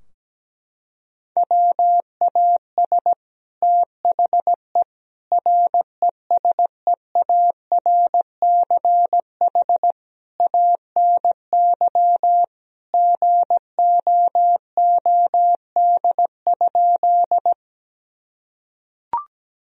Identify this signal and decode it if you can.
Morse code, heard as a keyed tone, about 17 wpm, 700 Hz. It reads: WAS THE RESEARCH ANY GOOD?
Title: Sentences from Top 1000 Words - Review Part 4 - 17wpm